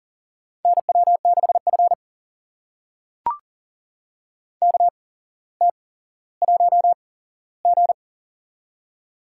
Morse code, heard as a keyed tone, about 40 wpm, spaced as for 17 wpm, 700 Hz. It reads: NW6F KT1G
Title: USA Call Signs - 4 Characters - 8x Character Spacing - 40wpm